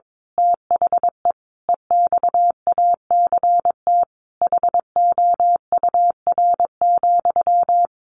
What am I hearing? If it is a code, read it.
Morse code, heard as a keyed tone, about 22 wpm, 700 Hz. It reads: THE EXACT HOUR,
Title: A Sound Of Thunder 22wpm